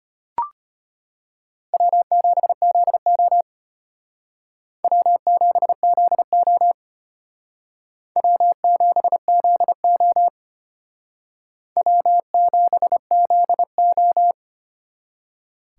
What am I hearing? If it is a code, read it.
Morse code, heard as a keyed tone, about 35 wpm, 700 Hz. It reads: W7ZO W7ZO W7ZO W7ZO